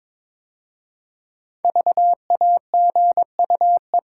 Morse code, heard as a keyed tone, about 22 wpm, 700 Hz. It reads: VAGUE
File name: 5 Letter Words - First 3 Letters Given - 22wpm